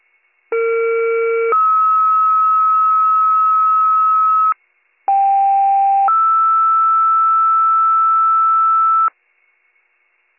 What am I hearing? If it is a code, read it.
Two-tone page, heard as a one-second tone, then a three-second tone, about 460 then 1290 Hz, about 790 then 1320 Hz. It is heard from 0.5 to 9.1 s.